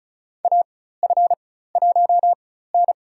Morse code, heard as a keyed tone, about 36 wpm, spaced as for 16 wpm, 700 Hz. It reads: AF1N
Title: USA Call Signs - 4 Characters - 4x Character Spacing - 35wpm